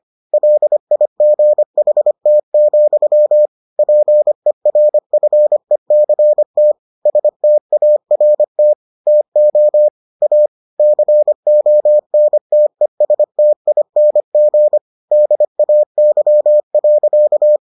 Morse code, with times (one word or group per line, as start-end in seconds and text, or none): 0.3-3.5 LIGHT,
3.8-6.7 PERFECT
7.1-8.7 START
9.1-9.9 TO
10.2-10.5 A
10.8-14.8 CONTESTING
15.1-17.6 DAY.